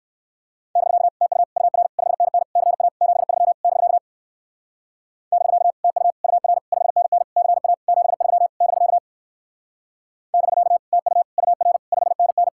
Morse code, heard as a keyed tone, about 64 wpm, 700 Hz. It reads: <BK> TU UR 5NN CA 73 <BK> <BK> TU UR 5NN CA 73 <BK> <BK> TU UR 5NN